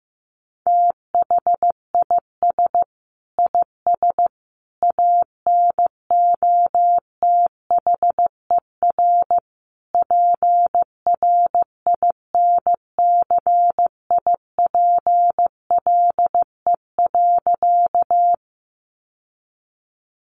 Morse code, heard as a keyed tone, about 15 wpm, 700 Hz. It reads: THIS IS ANOTHER PRINCIPLE.
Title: Sentences from Top 1400 Words 15wpm